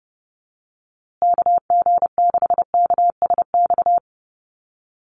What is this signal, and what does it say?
Morse code, heard as a keyed tone, about 30 wpm, 700 Hz. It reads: KG6KSX
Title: USA Call Signs Repeated 3x 30wpm